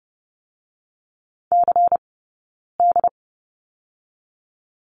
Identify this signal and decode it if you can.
Morse code, heard as a keyed tone, about 30 wpm, 700 Hz. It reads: C D
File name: Lesson 16d - Letter c - Two Characters 30wpm